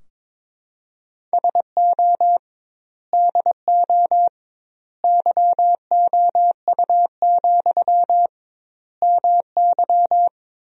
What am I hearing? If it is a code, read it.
Morse code, heard as a keyed tone, about 22 wpm, 700 Hz. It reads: SO DO YOU, MY
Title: 2x Word-Spacing - Sentences from Top 400 Words - 22wpm